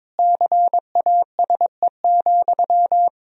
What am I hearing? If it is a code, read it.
Morse code, heard as a keyed tone, about 22 wpm, 700 Hz. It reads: CASE,
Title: Sentences from Top 1000 Words - Review Part 2 - 22wpm